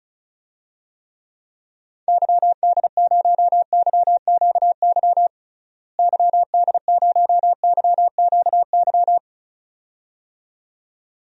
Morse code, heard as a keyed tone, about 34 wpm, 700 Hz. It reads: YD0YQY YD0YQY
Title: DXCC-100 Call Signs - 6 Characters - Repeated 2x - 35wpm